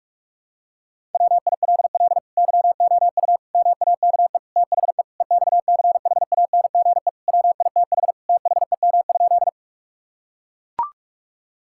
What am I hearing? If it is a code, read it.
Morse code, heard as a keyed tone, about 41 wpm, 700 Hz. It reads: WILL YOU MAKE THE EXCHANGE WITH THEM?
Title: Sentences from Top 1000 Words - Review Part 6 - 45wpm